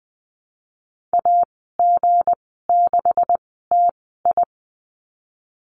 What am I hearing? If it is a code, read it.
Morse code, heard as a keyed tone, about 20 wpm, spaced as for 15 wpm, 700 Hz. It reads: AG6TI
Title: 173 Lesson 28g - Number 9 - Call Signs 20wpm